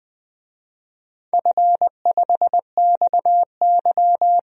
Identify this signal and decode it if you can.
Morse code, heard as a keyed tone, about 20 wpm, 700 Hz. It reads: F5XY
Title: DXCC-100 Call Signs - 4 Characters - 20wpm